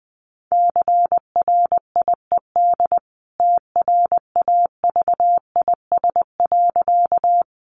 Morse code, heard as a keyed tone, about 20 wpm, 700 Hz. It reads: CRIED TRAVIS.
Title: A Sound Of Thunder 20wpm